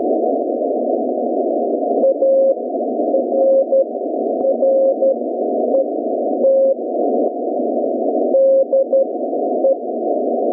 Morse code, heard as a keyed tone, about 12 wpm, spaced as for 9 wpm, 540 Hz. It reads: ARRET DE